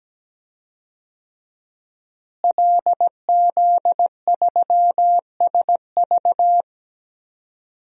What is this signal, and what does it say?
Morse code, heard as a keyed tone, about 17 wpm, 700 Hz. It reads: LZ3SV
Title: DXCC-100 Call Signs - 5 Characters - Repeated 2x - 17wpm